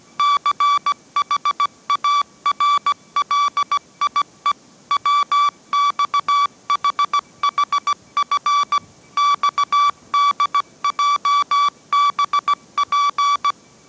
Morse code, heard as a keyed tone, about 17 wpm, 1150 Hz. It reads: CHARLIE WXHHF XDJBP
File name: mrx / mrx080 18wpm